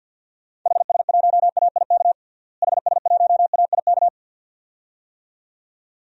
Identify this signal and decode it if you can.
Morse code, heard as a keyed tone, about 47 wpm, 700 Hz. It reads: HS1AIK HS1AIK
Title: DXCC-100 Call Signs - 6 Characters - Repeated 2x - 50wpm